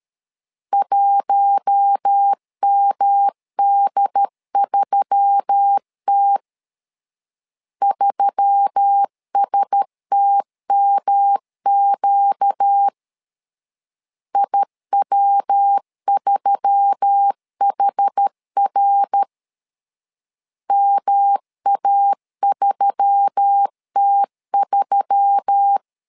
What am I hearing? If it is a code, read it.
Morse code, heard as a keyed tone, about 13 wpm, 800 Hz. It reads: 1MD3T 3STMQ IW3HR MA3T3
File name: LEZIONI / 4 Q1S3 12wpm 2023